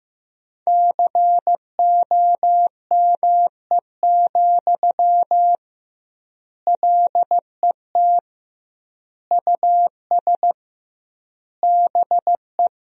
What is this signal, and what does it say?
Morse code, heard as a keyed tone, about 15 wpm, 700 Hz. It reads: COME, LET US BE
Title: 2x Word-Spacing - Sentences from Top 300 Words - 15wpm